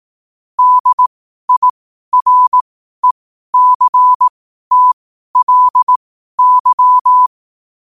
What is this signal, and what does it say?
Morse code, heard as a keyed tone, about 18 wpm, spaced as for 12 wpm, 1000 Hz. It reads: DIRECTLY
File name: qotdfiles / 12wpm